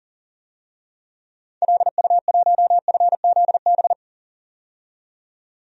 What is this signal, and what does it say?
Morse code, heard as a keyed tone, about 37 wpm, 700 Hz. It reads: LU1FZB
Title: DXCC-100 Call Signs - 6 Characters - 40wpm